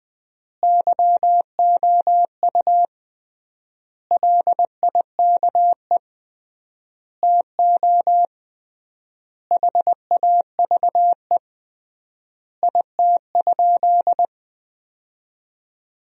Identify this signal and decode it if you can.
Morse code, heard as a keyed tone, about 21 wpm, 700 Hz. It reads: YOU LIKE TO HAVE IT?
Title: 3x Word-Spacing - Sentences from Top 100 Words - 20wpm